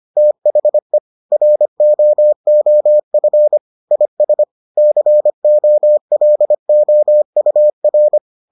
Morse code, heard as a keyed tone, about 25 wpm, 600 Hz. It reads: THE ROOF IS COLOUR